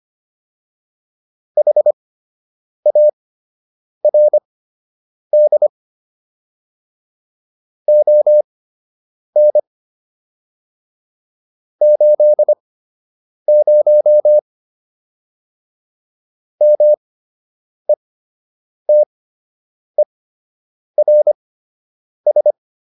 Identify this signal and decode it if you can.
Morse code, heard as a keyed tone, about 25 wpm, spaced as for 8 wpm, 600 Hz. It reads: HARD ON 80 METERS